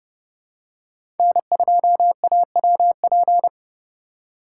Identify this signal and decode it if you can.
Morse code, heard as a keyed tone, about 29 wpm, 700 Hz. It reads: N2AWP
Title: USA Call Signs - 5 Characters - 3x Character Spacing - 30wpm